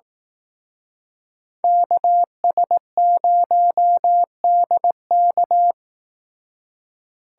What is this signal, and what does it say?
Morse code, heard as a keyed tone, about 18 wpm, 700 Hz. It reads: KS0DK
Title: Speed Racing - USA 2x2 Call Signs 15wpm